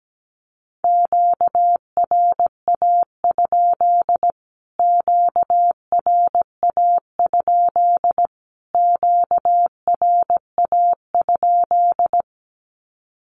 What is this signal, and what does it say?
Morse code, heard as a keyed tone, about 17 wpm, 700 Hz. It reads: QRA? QRA? QRA?
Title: Q-Codes Repeated 3x 17wpm